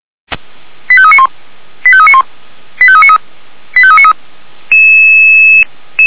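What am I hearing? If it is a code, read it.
Five-tone selective call (ZVEI), heard as quick runs of five tones, tones of about 80 ms. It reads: 85391 85391 85393 85393
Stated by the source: Five-tone ZVEI selective call received by SDR, from audio.f85360.feuerwehr-orlamuende.org fire brigade server.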